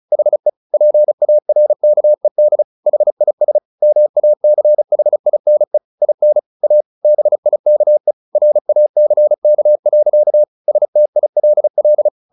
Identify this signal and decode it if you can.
Morse code, heard as a keyed tone, about 34 wpm, 600 Hz. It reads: HE PARKED HIS MACHINE IN A BIKE RACK. STILL